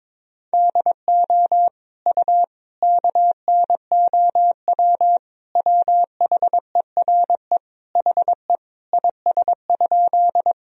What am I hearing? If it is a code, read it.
Morse code, heard as a keyed tone, about 22 wpm, 700 Hz. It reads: DO U KNOW WHERE HE IS?
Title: Sentences from Top 300 Words and Abbreviations 22wpm